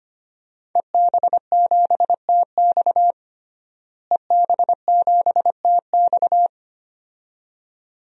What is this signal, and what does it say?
Morse code, heard as a keyed tone, about 25 wpm, 700 Hz. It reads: EB7TX EB7TX
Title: DXCC-100 Call Signs - 5 Characters - Repeated 2x - 25wpm